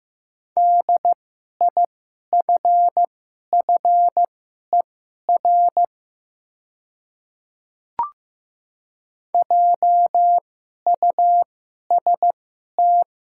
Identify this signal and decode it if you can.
Morse code, heard as a keyed tone, about 15 wpm, spaced as for 11 wpm, 690 Hz. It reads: DIFFER JUST
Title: 2x Character Spacing - Top 200 Words - 15wpm